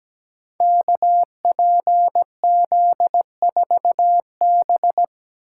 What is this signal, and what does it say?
Morse code, heard as a keyed tone, about 17 wpm, 700 Hz. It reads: KPZ4B
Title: Code Groups - 5 Letter:Numbers - 17wpm